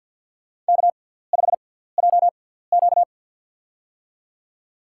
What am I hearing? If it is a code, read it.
Morse code, heard as a keyed tone, about 53 wpm, spaced as for 17 wpm, 700 Hz. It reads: K5JQ
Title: USA Call Signs - 4 Characters - 6x Character Spacing - 50wpm